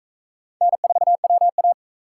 Morse code, heard as a keyed tone, about 43 wpm, 700 Hz. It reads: N4WA